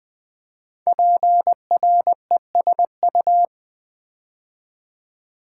Morse code, heard as a keyed tone, about 20 wpm, 700 Hz. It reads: PRESU
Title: Build A Word - 5 Letters At A Time - 10-Letter Words - 20wpm